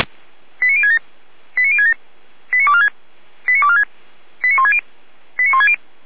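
Five-tone selective call (ZVEI), heard as quick runs of five tones, tones of about 70 ms. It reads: 89067 89067 89246 89246 89160 89160
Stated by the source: ZVEI five-tone selective call received by SDR.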